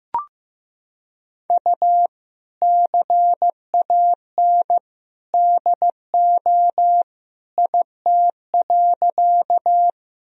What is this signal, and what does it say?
Morse code, heard as a keyed tone, about 15 wpm, 700 Hz. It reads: U CAN DO IT.
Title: Sentences from Top 100 Words and Abbreviations 15wpm